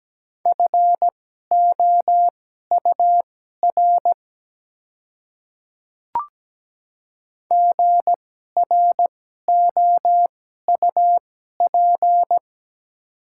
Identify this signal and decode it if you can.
Morse code, heard as a keyed tone, about 17 wpm, spaced as for 12 wpm, 700 Hz. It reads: FOUR GROUP
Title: 2x Character Spacing - Top 300 Words - 17wpm